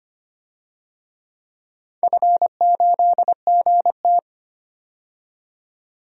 Morse code, heard as a keyed tone, about 25 wpm, 700 Hz. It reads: F8GT
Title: DXCC-100 Call Signs - 4 Characters - 25wpm